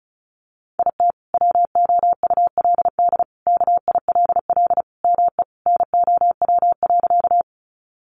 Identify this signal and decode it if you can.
Morse code, heard as a keyed tone, about 34 wpm, 700 Hz. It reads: IT WOULD KILL ME NOW.
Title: Sentences from Top 1400 Words 35wpm